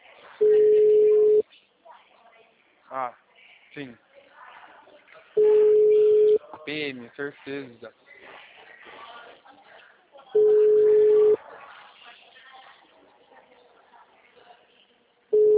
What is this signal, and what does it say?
Ringback tone (Europe, 425 Hz), with speech.
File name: out- -2021